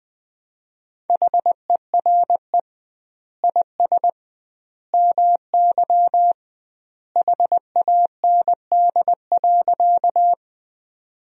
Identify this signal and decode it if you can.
Morse code, heard as a keyed tone, about 20 wpm, 700 Hz. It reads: HERE IS MY HAND.
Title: 2x Word-Spacing - Sentences from Top 200 Words - 20wpm